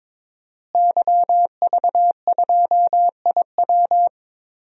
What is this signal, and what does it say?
Morse code, heard as a keyed tone, about 22 wpm, 700 Hz. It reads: YV2IW
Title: DXCC-100 Call Signs - 5 Characters - Repeated 2x - 22wpm